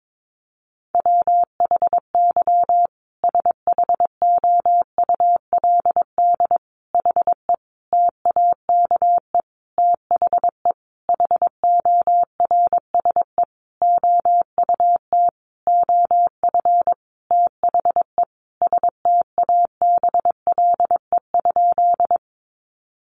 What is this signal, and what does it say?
Morse code, heard as a keyed tone, about 22 wpm, 700 Hz. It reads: WHY SHOULD HE TAKE THE HORSE OUT OF THE STABLE?
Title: Sentences from Top 1500 Words 22wpm